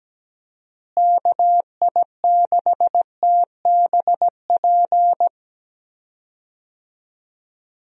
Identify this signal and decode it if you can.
Morse code, heard as a keyed tone, about 17 wpm, 700 Hz. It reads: KI6TBP
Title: USA Call Signs - 6 Characters - Repeated 2x - 17wpm